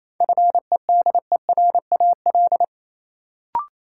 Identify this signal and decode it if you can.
Morse code, heard as a keyed tone, about 28 wpm, 700 Hz. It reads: FEDERAL
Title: Sets of 3 Words - New Word Drill - 28wpm